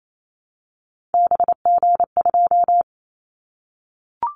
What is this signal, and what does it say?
Morse code, heard as a keyed tone, about 28 wpm, 700 Hz. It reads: BG2
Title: Callsign Prefixes for Top 100 DXCC - 28wpm